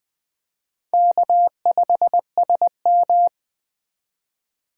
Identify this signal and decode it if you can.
Morse code, heard as a keyed tone, about 21 wpm, 700 Hz. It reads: K5SM
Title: Speed Racing - USA 1x2 Call Signs 17wpm